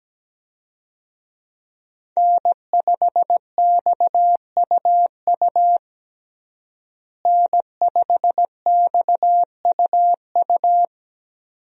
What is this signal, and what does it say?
Morse code, heard as a keyed tone, about 17 wpm, 700 Hz. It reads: N5XUU N5XUU
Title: USA Call Signs - 5 Characters - Repeated 2x - 17wpm